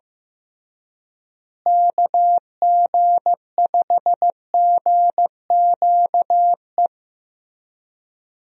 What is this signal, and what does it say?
Morse code, heard as a keyed tone, about 15 wpm, 700 Hz. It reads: KG5GQE
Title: USA Call Signs - 6 Characters - Repeated 3x - 15wpm